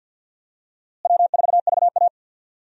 Morse code, heard as a keyed tone, about 51 wpm, 700 Hz. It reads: W4VA
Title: USA Call Signs - 4 Characters - Repeated 3x - 50wpm